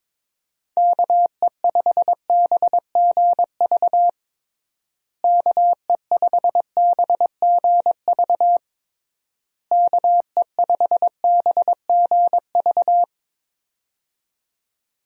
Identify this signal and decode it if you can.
Morse code, heard as a keyed tone, about 22 wpm, 700 Hz. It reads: KE5BGV KE5BGV KE5BGV
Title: USA Call Signs - 6 Characters - Repeated 3x - 22wpm